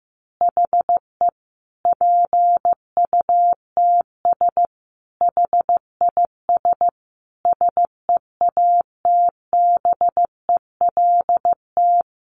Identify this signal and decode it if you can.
Morse code, heard as a keyed tone, about 15 wpm, 700 Hz. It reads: HE PUTS HIS SEATBELT
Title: Additional Easy Phrases 15wpm